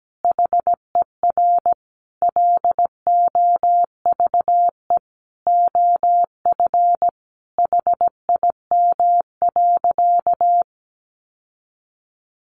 Morse code, heard as a keyed tone, about 17 wpm, 700 Hz. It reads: HER LOVE OF HIM.
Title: Sentences from Top 1500 Words 17wpm